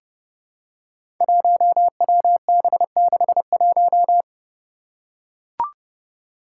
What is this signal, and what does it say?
Morse code, heard as a keyed tone, about 30 wpm, 700 Hz. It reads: 1WB61
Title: Five Challenging Letter-Number Combinations 30wpm